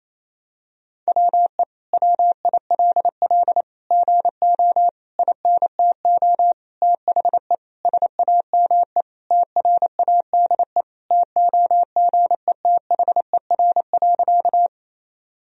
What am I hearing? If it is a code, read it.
Morse code, heard as a keyed tone, about 28 wpm, 700 Hz. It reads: WE WILL GO INTO THE SAME TRADE TOGETHER.
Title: Sentences from Top 1000 Words - Review Part 4 - 28wpm